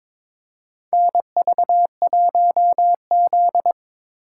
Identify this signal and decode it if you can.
Morse code, heard as a keyed tone, about 23 wpm, 700 Hz. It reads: NV1Z
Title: USA Call Signs - 4 Characters - 3x Character Spacing - 22wpm